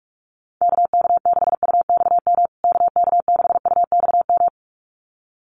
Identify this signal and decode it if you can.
Morse code, heard as a keyed tone, about 44 wpm, 700 Hz. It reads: KK6UXM KK6UXM
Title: USA Call Signs Repeated 2x 45wpm